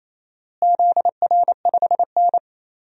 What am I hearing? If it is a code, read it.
Morse code, heard as a keyed tone, about 27 wpm, 700 Hz. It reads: ZR5N
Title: DXCC-100 Call Signs - 4 Characters - Repeated 3x - 28wpm